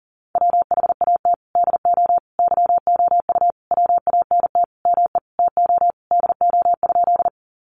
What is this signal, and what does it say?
Morse code, heard as a keyed tone, about 40 wpm, 700 Hz. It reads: WHAT DO YOU WANT ME TO DO?
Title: Sentences from Top 200 Words 40wpm